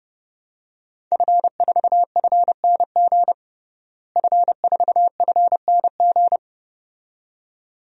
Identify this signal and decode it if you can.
Morse code, heard as a keyed tone, about 32 wpm, 700 Hz. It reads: F4FNG F4FNG
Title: DXCC-100 Call Signs - 5 Characters - Repeated 2x - 30wpm